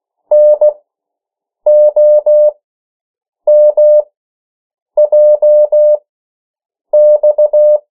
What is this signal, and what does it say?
Morse code, heard as a keyed tone, about 16 wpm, spaced as for 7 wpm, 600 Hz. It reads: NOMJX
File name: morse mp3 / 14 urok 13